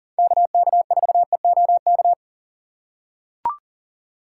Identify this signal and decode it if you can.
Morse code, heard as a keyed tone, about 38 wpm, 700 Hz. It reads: KK4EOK
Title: USA Call Signs - 4x Character Spacing - 40wpm